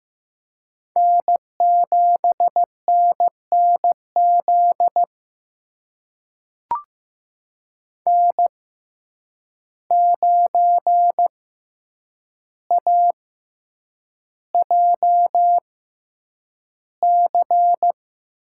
Morse code, heard as a keyed tone, about 15 wpm, spaced as for 8 wpm, 700 Hz. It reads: N7NNZ N9AJC
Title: USA Call Signs - 5 Characters - 6x Character Spacing - 15wpm